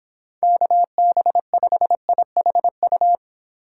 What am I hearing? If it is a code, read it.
Morse code, heard as a keyed tone, about 31 wpm, 700 Hz. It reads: KB5IHU